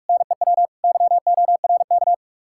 Morse code, heard as a keyed tone, about 62 wpm, spaced as for 53 wpm, 700 Hz. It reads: NEW YORK
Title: Speed-Racing Common QSO Phrases - 45wpm